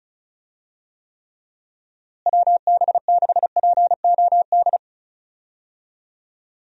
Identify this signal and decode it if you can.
Morse code, heard as a keyed tone, about 34 wpm, 700 Hz. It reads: WB6POD
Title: USA Call Signs - 6 Characters - Repeated 3x - 35wpm